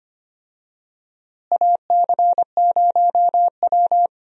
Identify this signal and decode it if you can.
Morse code, heard as a keyed tone, about 25 wpm, 700 Hz. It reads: AC0W
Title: USA Call Signs - 4 Characters - Repeated 3x - 25wpm